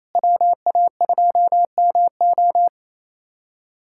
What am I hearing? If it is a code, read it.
Morse code, heard as a keyed tone, about 28 wpm, 700 Hz. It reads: WA2MO
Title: USA Callsign DE Callsign - 5 Characters - 28wpm